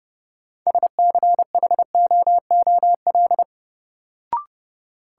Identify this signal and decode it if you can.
Morse code, heard as a keyed tone, about 30 wpm, 700 Hz. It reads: SCHOOL
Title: Sets of 2 Words - New Word Drill - 30wpm